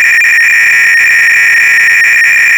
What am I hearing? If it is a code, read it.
300-baud modem data.